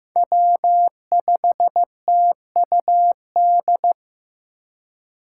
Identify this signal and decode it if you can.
Morse code, heard as a keyed tone, about 18 wpm, 700 Hz. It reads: W5TUD
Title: Speed Racing - USA 1x3 Call Signs 15wpm